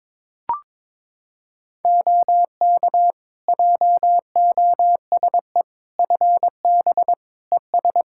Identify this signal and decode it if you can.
Morse code, heard as a keyed tone, about 22 wpm, 700 Hz. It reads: OK JOSE FB ES
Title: K1USN SST QSO Practice 22wpm